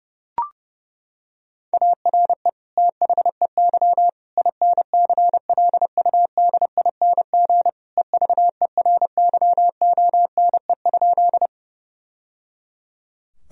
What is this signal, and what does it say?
Morse code, heard as a keyed tone, about 30 wpm, 700 Hz. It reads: ARE THEY INCLUDING EVERYONE?
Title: Sentences from Top 1000 Words - Review Part 4 - 30wpm